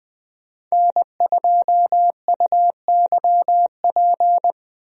Morse code, heard as a keyed tone, about 20 wpm, 700 Hz. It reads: N2UYP
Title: USA Call Signs - 5 Characters - 20wpm